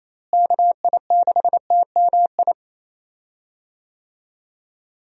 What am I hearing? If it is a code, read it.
Morse code, heard as a keyed tone, about 28 wpm, 700 Hz. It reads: KI6TMI
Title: USA Call Signs - 6 Characters - Repeated 3x - 28wpm